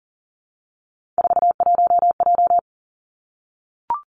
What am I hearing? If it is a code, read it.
Morse code, heard as a keyed tone, about 40 wpm, 700 Hz. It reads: 41J
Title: Three Challenging Letter-Number Combinations 40wpm